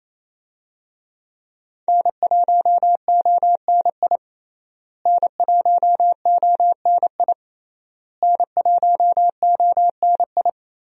Morse code, heard as a keyed tone, about 28 wpm, 700 Hz. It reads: N1ONI N1ONI N1ONI
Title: USA Call Signs - 5 Characters - Repeated 3x - 28wpm